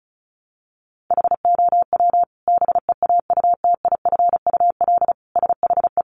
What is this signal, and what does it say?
Morse code, heard as a keyed tone, about 34 wpm, 700 Hz. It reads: HOW BEAUTIFUL SHE
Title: Sentences from Top 900 Words 35wpm